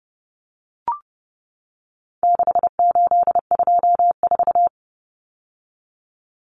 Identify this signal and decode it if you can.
Morse code, heard as a keyed tone, about 30 wpm, 700 Hz. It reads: 6824
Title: Common Cut Numbers - 4-Digit 30wpm